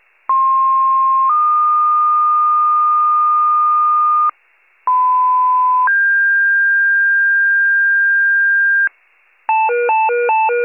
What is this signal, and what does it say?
Two-tone page, heard as a one-second tone, then a three-second tone, about 1060 then 1220 Hz, about 1010 then 1600 Hz, with noise. It is heard from 0.3 to 8.9 s.
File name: pager / 98 RESCUE / wcalert